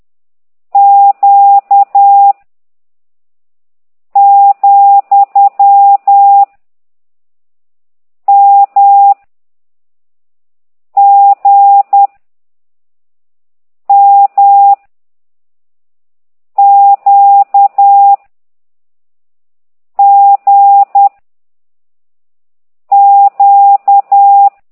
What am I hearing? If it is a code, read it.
Morse code, heard as a keyed tone, about 10 wpm, 800 Hz. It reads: Q , M G M Q G Q
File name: ct2idw / exerc15